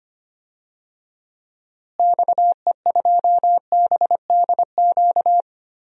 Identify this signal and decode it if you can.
Morse code, heard as a keyed tone, about 25 wpm, 700 Hz. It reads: XE2BDQ
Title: DXCC-100 Call Signs - 6 Characters - Repeated 2x - 25wpm